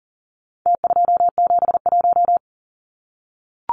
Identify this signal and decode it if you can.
Morse code, heard as a keyed tone, about 40 wpm, 700 Hz. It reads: T271
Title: Numbers - Contest Style - 4-Digit 40wpm